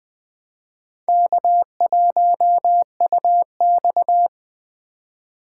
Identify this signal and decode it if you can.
Morse code, heard as a keyed tone, about 21 wpm, 700 Hz. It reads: K1UX
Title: Speed Racing - USA 1x2 Call Signs 17wpm